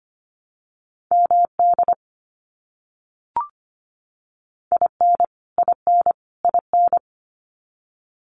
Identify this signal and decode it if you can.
Morse code, heard as a keyed tone, about 25 wpm, 700 Hz. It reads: MD IN IN IN
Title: US States Repeated 3x 25wpm